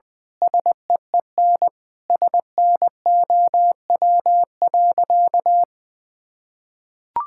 Morse code, heard as a keyed tone, about 20 wpm, 700 Hz. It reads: SEEN SNOW.
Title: Sentences with 8-Word Phrases - 20wpm